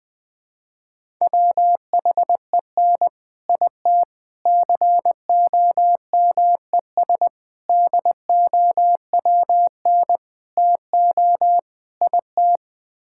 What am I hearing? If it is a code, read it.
Morse code, heard as a keyed tone, about 21 wpm, 700 Hz. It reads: WHEN IT COMES DOWN TO IT
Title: Sets of 6 Words - 20wpm